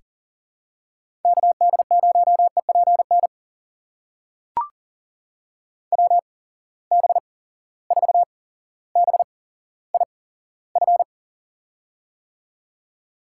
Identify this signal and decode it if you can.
Morse code, heard as a keyed tone, about 38 wpm, spaced as for 17 wpm, 700 Hz. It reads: KD0EPN WB4BIF
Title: USA Call Signs - 8x Character Spacing - 40wpm